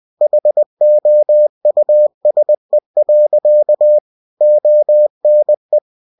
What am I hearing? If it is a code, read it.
Morse code, heard as a keyed tone, about 20 wpm, 600 Hz. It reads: HOUSE. ONE
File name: cw-books / winnie-the-pooh / winnie-20wpm-page-0015